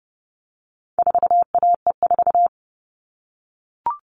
Morse code, heard as a keyed tone, about 30 wpm, 700 Hz. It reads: 4AE4
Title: All Cut Numbers - 4-Digit 30wpm